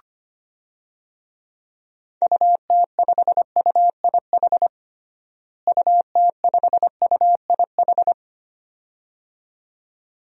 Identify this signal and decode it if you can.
Morse code, heard as a keyed tone, about 25 wpm, 700 Hz. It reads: UT5UIH UT5UIH
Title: DXCC-100 Call Signs - 6 Characters - Repeated 2x - 25wpm